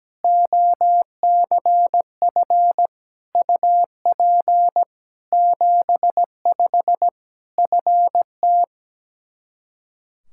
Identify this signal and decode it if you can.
Morse code, heard as a keyed tone, about 17 wpm, 700 Hz. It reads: OCF UP 75 FT
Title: K1USN SST QSO Practice 17wpm